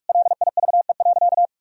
Morse code, heard as a keyed tone, about 46 wpm, 700 Hz. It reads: LIVE.
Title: Sentences from Top 1000 Words - Review Part 3 - 45wpm